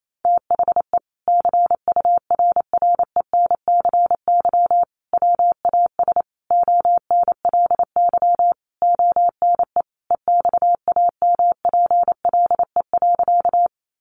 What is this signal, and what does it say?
Morse code, heard as a keyed tone, about 28 wpm, 700 Hz. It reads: THE CURRENCY WAS ONLY ONE EXAMPLE.
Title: Sentences from Top 1500 Words 28wpm